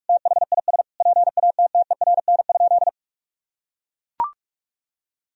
Morse code, heard as a keyed tone, about 41 wpm, 700 Hz. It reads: THIS PATTERN?
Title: Sentences from Top 1000 Words - Review Part 2 - 45wpm